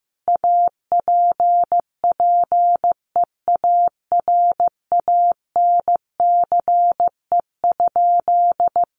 Morse code, heard as a keyed tone, about 15 wpm, 700 Hz. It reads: APPEARANCE?
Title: Sentences from Top 1100 Words 15wpm